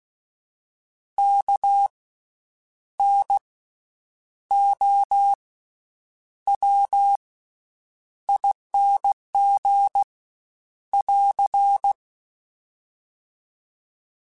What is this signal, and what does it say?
Morse code, heard as a keyed tone, about 16 wpm, spaced as for 8 wpm, 800 Hz. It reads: KNOWING <AR>
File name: Special INGs / Special INGs-mp3s / VerySlowToFaster Ing @15